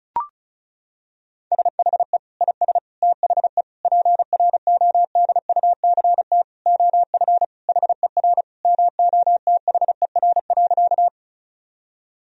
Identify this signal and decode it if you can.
Morse code, heard as a keyed tone, about 36 wpm, 700 Hz. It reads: SHE IS THE PRODUCT OF HER MOTHER.
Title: Sentences from Top 1000 Words - Review Part 2 - 35wpm